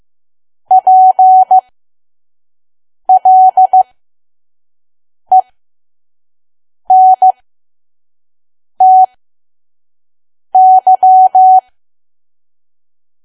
Morse code, heard as a keyed tone, about 15 wpm, spaced as for 5 wpm, 750 Hz. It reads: PLENTY
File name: Archive / 5 WPM files / PM